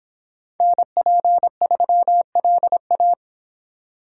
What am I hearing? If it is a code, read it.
Morse code, heard as a keyed tone, about 26 wpm, 700 Hz. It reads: NP3LA